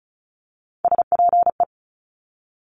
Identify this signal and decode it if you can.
Morse code, heard as a keyed tone, about 36 wpm, 700 Hz. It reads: SPE
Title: Top Three Letter Combinations in Words 35wpm